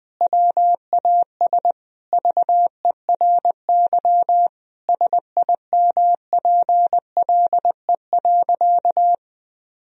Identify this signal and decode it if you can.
Morse code, heard as a keyed tone, about 21 wpm, 700 Hz. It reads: WAS VERY SIMPLE.
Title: Sentences from Top 1000 Words - Review Part 4 - 20wpm